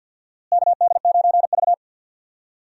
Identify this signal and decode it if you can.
Morse code, heard as a keyed tone, about 52 wpm, 700 Hz. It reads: KD9V
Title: USA Call Signs - 4 Characters - Repeated 2x - 50wpm